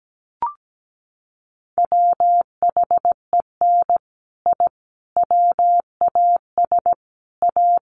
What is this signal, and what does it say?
Morse code, heard as a keyed tone, about 17 wpm, 700 Hz. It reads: WHEN I WAS A
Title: Sets of 5 Words Encore 17wpm